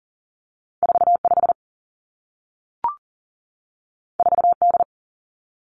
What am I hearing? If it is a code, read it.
Morse code, heard as a keyed tone, about 41 wpm, 700 Hz. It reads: 4 5 4 D
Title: Two Challenging Letter-Number Combinations 40wpm